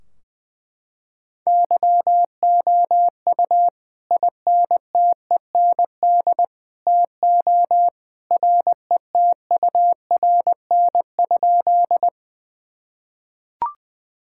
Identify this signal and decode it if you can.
Morse code, heard as a keyed tone, about 20 wpm, 700 Hz. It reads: YOU INTEND TO RETURN?
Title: Sentences from Top 1600 Words 20wpm